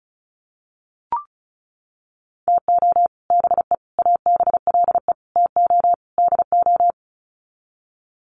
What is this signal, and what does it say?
Morse code, heard as a keyed tone, about 34 wpm, 700 Hz. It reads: TO BE ABLE TO DO
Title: Sets of 5 Words 35wpm